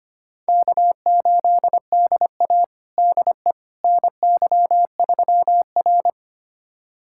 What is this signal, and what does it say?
Morse code, heard as a keyed tone, about 25 wpm, 700 Hz. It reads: K8DA DE NY3R
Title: USA Callsign DE Callsign - 4 Characters - 25wpm